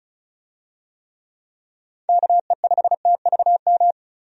Morse code, heard as a keyed tone, about 36 wpm, 700 Hz. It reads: KE5TVM
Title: USA Call Signs - 6 Characters - 35wpm